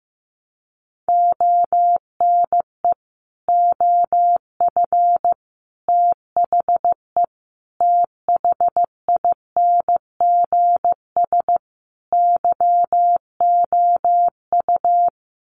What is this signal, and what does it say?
Morse code, heard as a keyed tone, about 15 wpm, 700 Hz. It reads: ONE OF THE THINGS YOU
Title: Sets of 5 Words Encore2 15wpm